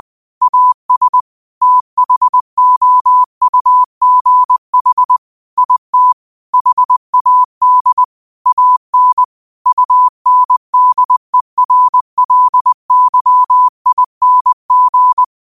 Morse code, heard as a keyed tone, about 20 wpm, 1000 Hz. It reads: AS THOUGH IT HAD AN UNDERLYING
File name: qotdfiles / 20wpm